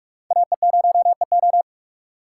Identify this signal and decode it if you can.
Morse code, heard as a keyed tone, about 43 wpm, 700 Hz. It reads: AE0EO
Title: USA Call Signs - 5 Characters - 3x Character Spacing - 45wpm